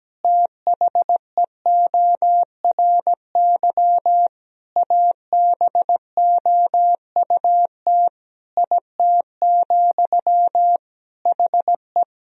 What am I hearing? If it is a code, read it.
Morse code, heard as a keyed tone, about 17 wpm, 700 Hz. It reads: THEORY ABOUT IT, HE
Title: Sentences from Top 1000 Words - Review Part 3 - 17wpm